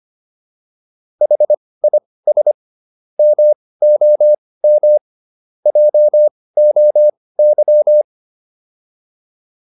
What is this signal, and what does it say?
Morse code, heard as a keyed tone, about 25 wpm, spaced as for 18 wpm, 600 Hz. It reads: HIS MOM JOY
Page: intermediate-practice-files